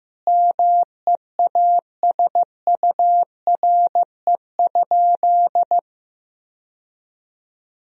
Morse code, heard as a keyed tone, about 15 wpm, 700 Hz. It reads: MEASURE?
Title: Sentences from Top 1000 Words - Review Part 3 - 15wpm